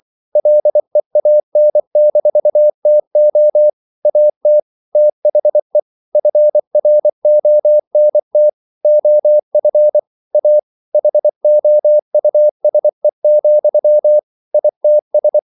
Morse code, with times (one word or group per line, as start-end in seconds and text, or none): 0.4-3.7 LEAN-TO
4.1-4.6 AT
5.0-5.8 THE
6.2-8.5 FRONT
8.8-10.0 OF
10.3-10.6 A
10.9-14.2 HOUSE,
14.5-15.4 ITS